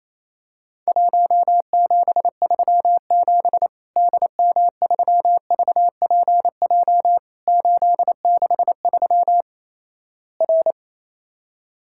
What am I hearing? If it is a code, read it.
Morse code, heard as a keyed tone, about 28 wpm, 700 Hz. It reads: 1737 DM3VPJ 863 R
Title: Worked All Europe DX Contest - QTC Exchange - 28wpm